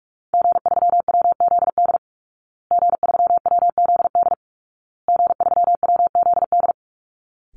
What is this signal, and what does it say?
Morse code, heard as a keyed tone, about 44 wpm, 700 Hz. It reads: G3WZD G3WZD G3WZD
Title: Call Signs Repeated 2x - 45wpm